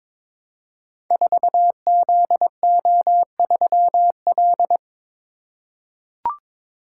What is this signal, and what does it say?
Morse code, heard as a keyed tone, about 22 wpm, 700 Hz. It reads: 4ZO3L
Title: Code Groups - 5 Letter:Numbers - 22wpm